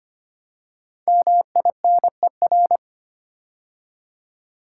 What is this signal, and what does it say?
Morse code, heard as a keyed tone, about 25 wpm, 700 Hz. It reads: MINER
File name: 5 Letter Words - First 3 Letters Given - 25wpm